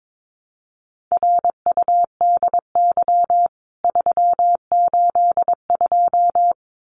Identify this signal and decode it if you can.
Morse code, heard as a keyed tone, about 22 wpm, 700 Hz. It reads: RUDY 382
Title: CWT Exchange 22wpm